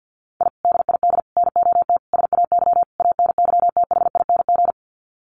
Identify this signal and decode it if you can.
Morse code, heard as a keyed tone, about 50 wpm, 700 Hz. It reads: I DID NOT SAY ANYTHING
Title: Sets of 5 Words Encore2 50wpm